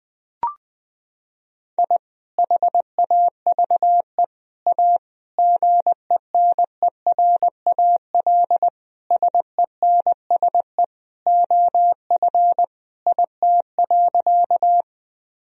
Morse code, heard as a keyed tone, about 20 wpm, 700 Hz. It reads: I HAVE A GENERAL SENSE OF IT.
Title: Sentences from Top 1000 Words - Review Part 4 - 20wpm